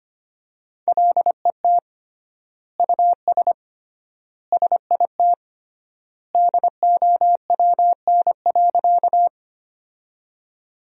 Morse code, heard as a keyed tone, about 25 wpm, 700 Hz. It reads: LET US SIT DOWN.
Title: 3x Word-Spacing - Sentences from Top 500 Words - 25wpm